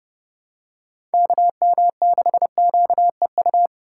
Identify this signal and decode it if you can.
Morse code, heard as a keyed tone, about 30 wpm, 700 Hz. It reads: KM6QEU
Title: USA Call Signs - 2x Character Spacing - 30wpm